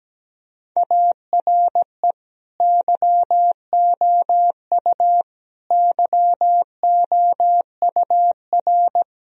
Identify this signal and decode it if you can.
Morse code, heard as a keyed tone, about 17 wpm, 700 Hz. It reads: ARE YOU YOUR
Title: Sentences from Top 1000 Words - Review Part 4 - 17wpm